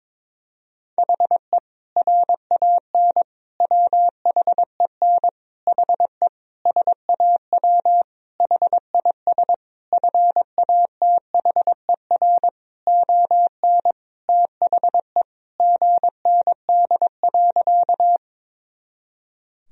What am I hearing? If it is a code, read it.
Morse code, heard as a keyed tone, about 21 wpm, 700 Hz. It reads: HE RAN WHEN HE SAW HIS FATHER ON THE GND.
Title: Sentences from Top 500 Words and Abbreviations 22wpm